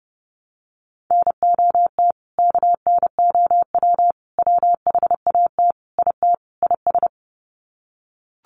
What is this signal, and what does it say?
Morse code, heard as a keyed tone, about 30 wpm, 700 Hz. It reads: NOT KNOW WHAT IT IS
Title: Sets of 5 Words Encore 30wpm